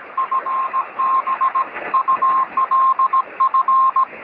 Morse code, heard as a keyed tone, about 17 wpm, 1060 Hz, with radio noise. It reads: FB ULF
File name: sm7lcb / 1911z dj8ms 23cm